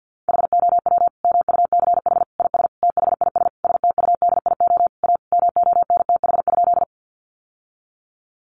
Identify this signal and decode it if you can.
Morse code, heard as a keyed tone, about 50 wpm, 700 Hz. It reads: HOW MUCH IS THIS STUDIO A MONTH?
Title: Sentences from Top 1200 Words 50wpm